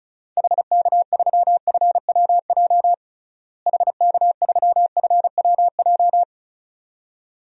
Morse code, heard as a keyed tone, about 34 wpm, 700 Hz. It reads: HK3FWJ HK3FWJ
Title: DXCC-100 Call Signs - 6 Characters - Repeated 2x - 35wpm